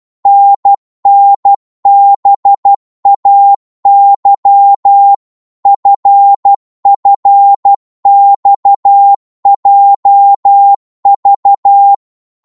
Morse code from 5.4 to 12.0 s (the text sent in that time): FFXJV